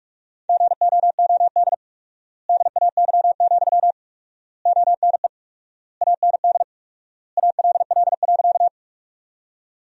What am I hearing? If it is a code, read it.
Morse code, heard as a keyed tone, about 42 wpm, 700 Hz. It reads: GOOD DAY, ONE AND ALL.
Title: 4x Word-Spacing - Sentences from Top 200 Words - 45wpm